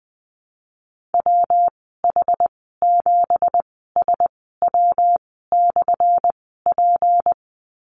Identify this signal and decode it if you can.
Morse code, heard as a keyed tone, about 20 wpm, spaced as for 15 wpm, 700 Hz. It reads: WH7SW/P
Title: 179 Lesson 29g - Slash - Call Signs 20wpm